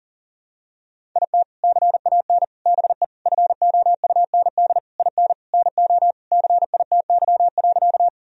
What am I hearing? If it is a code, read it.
Morse code, heard as a keyed tone, about 37 wpm, 700 Hz. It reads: IT CAN BE FOUND IN NO CITY.
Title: Sentences from Top 1000 Words - Review Part 1 - 40wpm